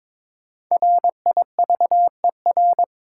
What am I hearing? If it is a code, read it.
Morse code, heard as a keyed tone, about 23 wpm, 690 Hz. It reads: RIVER
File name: 5 Letter Words - First 3 Letters Given - 22wpm